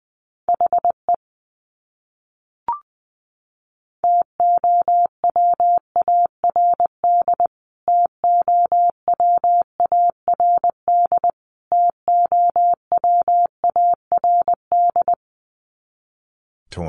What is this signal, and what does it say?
Morse code, heard as a keyed tone, about 19 wpm, 700 Hz. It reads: HE TOWARD TOWARD TOWARD
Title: Top 500 Words Repeated 20wpm